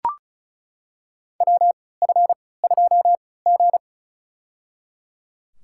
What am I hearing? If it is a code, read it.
Morse code, heard as a keyed tone, about 36 wpm, spaced as for 20 wpm, 700 Hz. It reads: WF2G
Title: USA Call Signs - 4 Characters - 3x Character Spacing - 35wpm